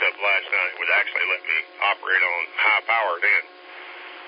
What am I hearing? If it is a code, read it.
Emergency Alert System (SAME) tones, heard as harsh buzzing bursts, with speech. It is heard from 0.7 to 3.5 s.